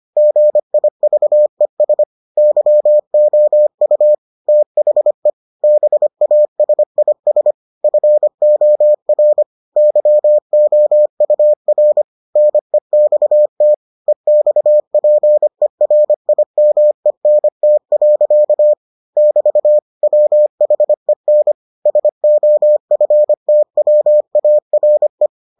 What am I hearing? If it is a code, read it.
Morse code, heard as a keyed tone, about 25 wpm, 600 Hz. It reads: GIVES YOU THE BASIS FOR YOUR NEXT EXPERIMENT. <BT> WHEN SOFTWARE